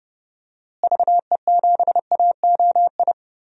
Morse code, heard as a keyed tone, about 30 wpm, 700 Hz. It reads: VE7AOI
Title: DXCC-100 Call Signs - 6 Characters - Repeated 3x - 30wpm